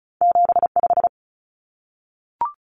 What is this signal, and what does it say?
Morse code, heard as a keyed tone, about 36 wpm, 700 Hz. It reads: 75
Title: Numbers - 2-Digit 35wpm